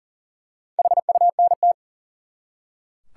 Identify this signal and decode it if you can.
Morse code, heard as a keyed tone, about 40 wpm, 700 Hz. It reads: HUNT
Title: Four Letter Words - 40wpm